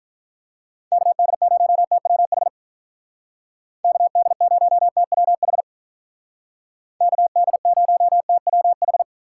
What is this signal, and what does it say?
Morse code, heard as a keyed tone, about 46 wpm, 700 Hz. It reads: KD0TWH KD0TWH KD0TWH